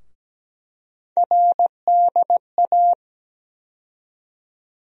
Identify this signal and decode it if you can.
Morse code, heard as a keyed tone, about 17 wpm, 700 Hz. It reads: RDA
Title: USA Call Sign Postfixes - 3 Letters 17wpm